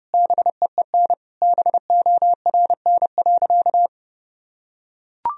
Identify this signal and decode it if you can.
Morse code, heard as a keyed tone, about 31 wpm, 700 Hz. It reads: BEEN BORN.
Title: Sentences from Top 1000 Words - Review Part 2 - 30wpm